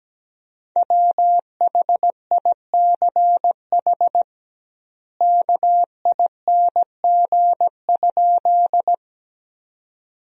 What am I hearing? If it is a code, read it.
Morse code, heard as a keyed tone, about 17 wpm, 700 Hz. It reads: WHICH KING?
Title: 2x Word-Spacing - Sentences from Top 400 Words - 17wpm